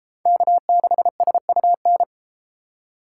Morse code, heard as a keyed tone, about 32 wpm, 700 Hz. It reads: K6SUN